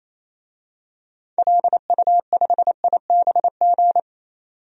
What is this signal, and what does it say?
Morse code, heard as a keyed tone, about 28 wpm, 700 Hz. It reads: LU5IBG
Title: DXCC-100 Call Signs - 6 Characters - 28wpm